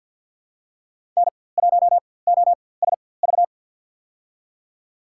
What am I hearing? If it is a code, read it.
Morse code, heard as a keyed tone, about 52 wpm, spaced as for 23 wpm, 700 Hz. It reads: N1OSV
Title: USA Call Signs - 4x Character Spacing - 50wpm